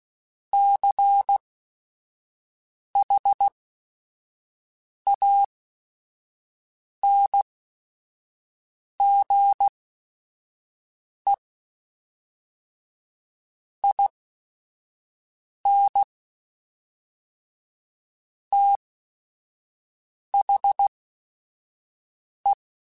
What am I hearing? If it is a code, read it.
Morse code, heard as a keyed tone, about 16 wpm, spaced as for 5 wpm, 800 Hz. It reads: CHANGE IN THE
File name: Technical Subjects / Technical Subjects-mp3s / XC @5